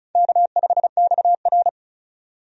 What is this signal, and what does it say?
Morse code, heard as a keyed tone, about 34 wpm, 700 Hz. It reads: K5XR
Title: USA Call Signs - 4 Characters - 35wpm